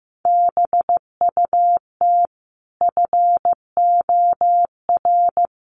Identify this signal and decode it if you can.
Morse code, heard as a keyed tone, about 14 wpm, 700 Hz. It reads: BUT FOR
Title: Sets of 5 Words Encore 15wpm